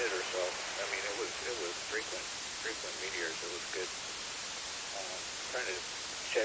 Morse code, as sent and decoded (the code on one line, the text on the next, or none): none